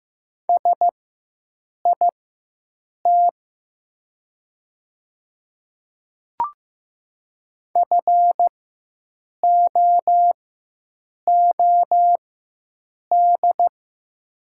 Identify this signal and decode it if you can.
Morse code, heard as a keyed tone, about 15 wpm, spaced as for 7 wpm, 700 Hz. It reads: SIT FOOD
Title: 4x Character Spacing - Top 500 Words - 15wpm